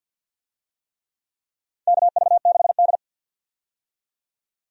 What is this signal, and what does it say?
Morse code, heard as a keyed tone, about 53 wpm, 700 Hz. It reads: KV6D
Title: USA Call Signs - 4 Characters - 50wpm